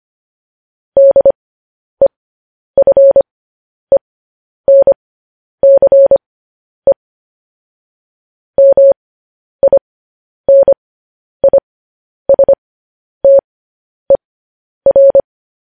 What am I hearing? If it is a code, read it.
Morse code, heard as a keyed tone, about 25 wpm, spaced as for 10 wpm, 550 Hz. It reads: DEFENCE MINISTER
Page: news headlines in morse code at 10 wpm